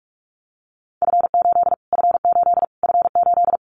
Morse code, heard as a keyed tone, about 44 wpm, 700 Hz. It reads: F8 F8 F8
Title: Callsign Prefixes for Top 25 DXCC - Repeated 3x - 45wpm